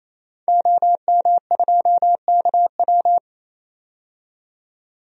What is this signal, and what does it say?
Morse code, heard as a keyed tone, about 28 wpm, 700 Hz. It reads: OM2KW
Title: DXCC-100 Call Signs - 5 Characters - 28wpm